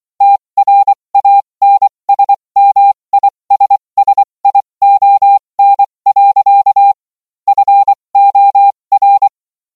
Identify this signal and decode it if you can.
Morse code, heard as a keyed tone, about 24 wpm, spaced as for 20 wpm, 800 Hz. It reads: TRANSMISSION. FOR